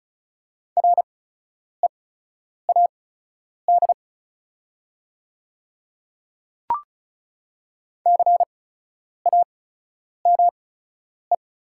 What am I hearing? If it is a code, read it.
Morse code, heard as a keyed tone, about 34 wpm, spaced as for 10 wpm, 690 Hz. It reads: READ CAME
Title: 8x Character Spacing - Top 200 Words - 35wpm